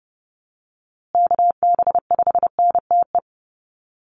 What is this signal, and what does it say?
Morse code, heard as a keyed tone, about 30 wpm, 700 Hz. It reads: KB5NTE
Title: USA Call Signs Repeated 3x 30wpm